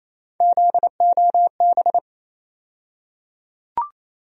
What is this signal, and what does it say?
Morse code, heard as a keyed tone, about 27 wpm, 700 Hz. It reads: ZOB
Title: USA Call Sign Postfixes - 3 Letters 28wpm